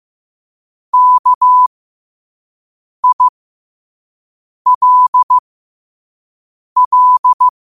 Morse code, heard as a keyed tone, about 15 wpm, spaced as for 5 wpm, 1000 Hz. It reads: KILL
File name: qotdfiles / 5wpm